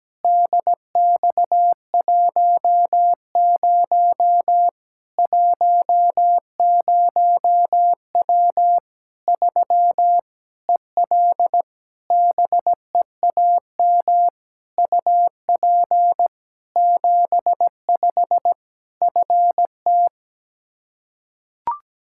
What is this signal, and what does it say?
Morse code, heard as a keyed tone, about 17 wpm, 700 Hz. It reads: DX10 10W 3 EL BEAM UP 75 FT
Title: K1USN SST QSO Practice 17wpm